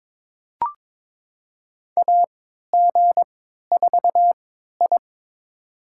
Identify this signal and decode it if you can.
Morse code, heard as a keyed tone, about 22 wpm, spaced as for 12 wpm, 700 Hz. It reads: AG4I
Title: USA Call Signs - 4 Characters - 3x Character Spacing - 22wpm